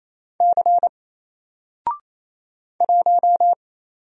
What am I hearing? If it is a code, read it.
Morse code, heard as a keyed tone, about 27 wpm, 700 Hz. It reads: C 1
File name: Single-Letter-Number-28wpm